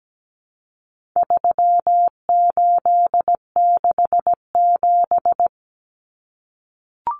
Morse code, heard as a keyed tone, about 17 wpm, 700 Hz. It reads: 3867
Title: Numbers - 4-Digit 17wpm